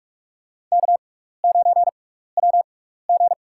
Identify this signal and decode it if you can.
Morse code, heard as a keyed tone, about 46 wpm, spaced as for 15 wpm, 700 Hz. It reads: K9WG
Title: USA Call Signs - 4 Characters - 6x Character Spacing - 45wpm